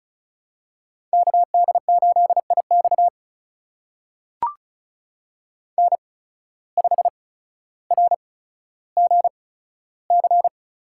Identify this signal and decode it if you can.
Morse code, heard as a keyed tone, about 36 wpm, spaced as for 15 wpm, 700 Hz. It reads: KD8IX N5RGC
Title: USA Call Signs - 5 Characters - 8x Character Spacing - 35wpm